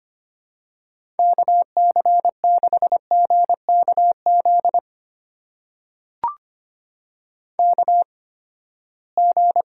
Morse code, heard as a keyed tone, about 25 wpm, spaced as for 17 wpm, 700 Hz. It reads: KC6GKZ KG
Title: USA Call Signs - 6 Characters - 8x Character Spacing - 25wpm